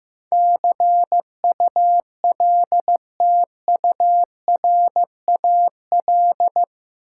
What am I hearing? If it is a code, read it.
Morse code, heard as a keyed tone, about 15 wpm, 700 Hz. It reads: CULTURAL
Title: Sentences from Top 1500 Words - New Word Drill - 15wpm